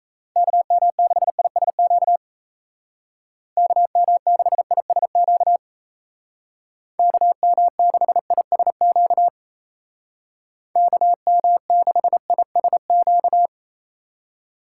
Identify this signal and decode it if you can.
Morse code, heard as a keyed tone, about 37 wpm, 700 Hz. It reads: KM6ISQ KM6ISQ KM6ISQ KM6ISQ